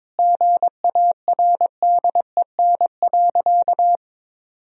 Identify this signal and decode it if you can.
Morse code, heard as a keyed tone, about 21 wpm, 700 Hz. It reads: GARDEN.
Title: Sentences from Top 1000 Words - Review Part 4 - 22wpm